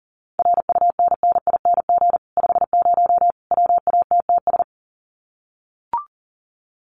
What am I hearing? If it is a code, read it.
Morse code, heard as a keyed tone, about 40 wpm, 700 Hz. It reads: RUNNING 50 WATTS
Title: Common QSO Phrases 40wpm